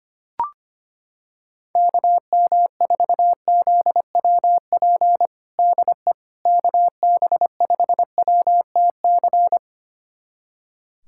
Morse code, heard as a keyed tone, about 25 wpm, 700 Hz. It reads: KM4ZWP DE KB5WTC
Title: USA Callsign DE Callsign 25wpm